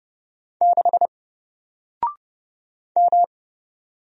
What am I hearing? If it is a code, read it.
Morse code, heard as a keyed tone, about 29 wpm, 700 Hz. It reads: 6 M
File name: Single-Letter-Number-30wpm